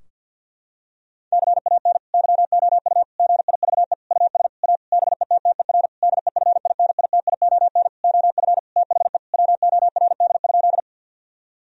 Morse code, heard as a keyed tone, about 50 wpm, 700 Hz. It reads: CAN YOU GIVE US A BETTER DEFINITION OF THE WORD?
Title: Sentences from Top 1000 Words - Review Part 4 - 50wpm